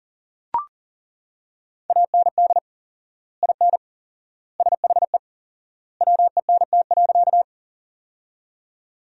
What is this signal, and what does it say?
Morse code, heard as a keyed tone, about 40 wpm, 700 Hz. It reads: AND IN SHE WENT.
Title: 4x Word-Spacing - Sentences from Top 200 Words - 40wpm